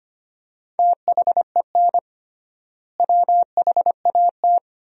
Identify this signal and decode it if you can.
Morse code, heard as a keyed tone, about 25 wpm, 700 Hz. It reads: THEN WHAT
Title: 3x Word-Spacing - Sentences from Top 200 Words - 25wpm